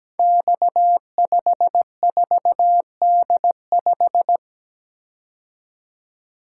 Morse code, heard as a keyed tone, about 17 wpm, 700 Hz. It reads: X54D5
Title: Code Groups - 5 Letter:Numbers - 17wpm